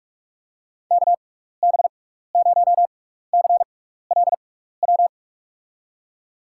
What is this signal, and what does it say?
Morse code, heard as a keyed tone, about 41 wpm, spaced as for 15 wpm, 700 Hz. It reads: KB0CLW
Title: USA Call Signs - 6 Characters - 6x Character Spacing - 45wpm